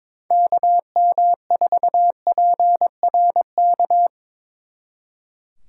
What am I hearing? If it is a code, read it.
Morse code, heard as a keyed tone, about 24 wpm, 700 Hz. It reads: KM4PRK
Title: Speed Racing - USA Call Signs 22wpm